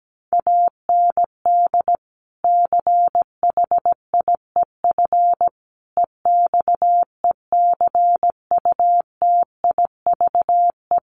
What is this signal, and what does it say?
Morse code, heard as a keyed tone, about 17 wpm, 700 Hz. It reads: AND CHIEF EXECUTIVE
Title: Sets of 5 Words Encore2 17wpm